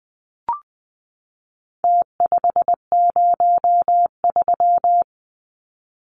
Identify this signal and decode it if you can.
Morse code, heard as a keyed tone, about 20 wpm, 700 Hz. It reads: T503
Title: Numbers - Contest Style - 4-Digit 20wpm